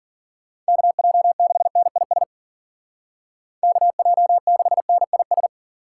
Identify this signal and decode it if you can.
Morse code, heard as a keyed tone, about 53 wpm, 700 Hz. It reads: KJ6NIS KJ6NIS